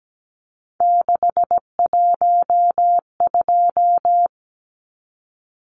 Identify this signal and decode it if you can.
Morse code, heard as a keyed tone, about 17 wpm, 700 Hz. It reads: 612
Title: Numbers - 3-Digit 17wpm